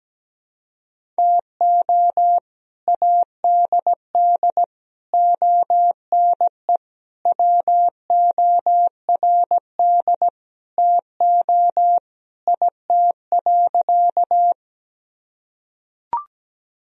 Morse code, heard as a keyed tone, about 17 wpm, 700 Hz. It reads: TO ADD ONE WORD TO IT.
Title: Sentences from Top 1000 Words - Review Part 1 - 17wpm